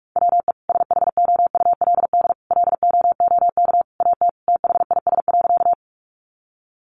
Morse code, heard as a keyed tone, about 44 wpm, 700 Hz. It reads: WE SHOULD LOOK AT THIS.
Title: Sentences from Top 300 Words 45wpm